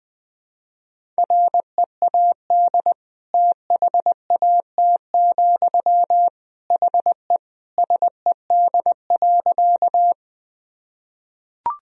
Morse code, heard as a keyed tone, about 20 wpm, 700 Hz. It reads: READ THAT, HE SED.
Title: Sentences from Top 200 Words and Abbreviations 20wpm